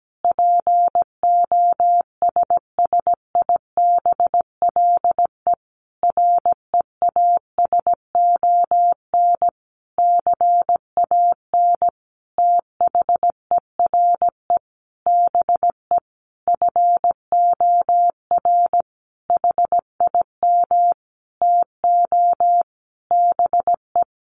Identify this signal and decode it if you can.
Morse code, heard as a keyed tone, about 17 wpm, 700 Hz. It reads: POSSIBLE REASON CAN THERE BE FOR HIM TO BE
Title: Sentences from Top 1300 Words 17wpm